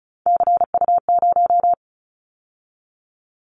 Morse code, heard as a keyed tone, about 34 wpm, 700 Hz. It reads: CU0
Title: Callsign Prefixes for Top 100 DXCC - 35wpm